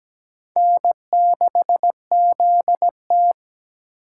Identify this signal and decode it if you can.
Morse code, heard as a keyed tone, about 20 wpm, 700 Hz. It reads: N6ZT
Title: Speed Racing - USA 1x2 Call Signs 17wpm